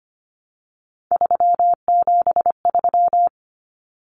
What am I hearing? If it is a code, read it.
Morse code, heard as a keyed tone, about 25 wpm, 700 Hz. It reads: 373
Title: Numbers - 3-Digit 25wpm